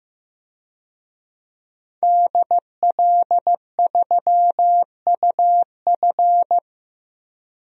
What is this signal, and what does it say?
Morse code, heard as a keyed tone, about 15 wpm, 700 Hz. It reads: DL3UF
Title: DXCC-100 Call Signs - 5 Characters - 15wpm